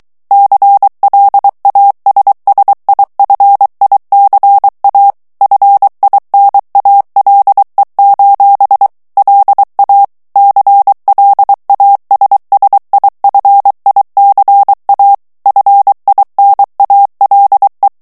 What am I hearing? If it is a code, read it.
Morse code, heard as a keyed tone, about 23 wpm, 800 Hz. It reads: CLASSIFICA FINALE: LA CLASSIFICA FINALE
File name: ITALIEN texte clair 2 vitesse 22 mots minute